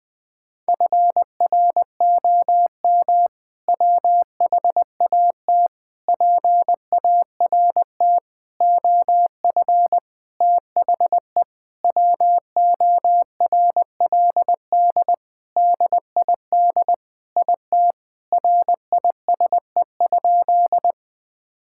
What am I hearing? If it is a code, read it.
Morse code, heard as a keyed tone, about 20 wpm, 700 Hz. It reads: FROM WHAT PART OF THE WORLD DID IT RISE?
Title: Sentences from Top 1000 Words - Review Part 6 - 20wpm